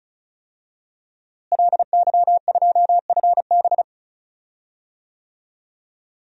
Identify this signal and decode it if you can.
Morse code, heard as a keyed tone, about 36 wpm, 700 Hz. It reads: LY2FB
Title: DXCC-100 Call Signs - 5 Characters - Repeated 2x - 35wpm